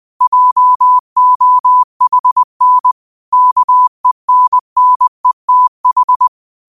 Morse code, heard as a keyed tone, about 20 wpm, 1000 Hz. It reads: JOHN KENNETH
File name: qotdfiles / 20wpm